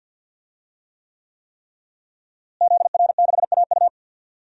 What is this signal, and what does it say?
Morse code, heard as a keyed tone, about 52 wpm, 700 Hz. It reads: ZR6AU
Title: DXCC-100 Call Signs - 5 Characters - Repeated 3x - 50wpm